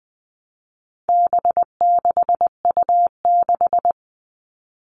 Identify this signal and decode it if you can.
Morse code, heard as a keyed tone, about 20 wpm, 700 Hz. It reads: B6U6
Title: All Cut Numbers - 4-Digit 20wpm